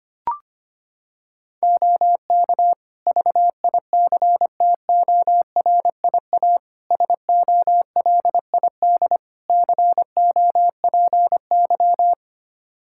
Morse code, heard as a keyed tone, about 25 wpm, 700 Hz. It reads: OK VICTORIA SOLID COPY
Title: Practice QSO - Exchange 3 - More Info - 25wpm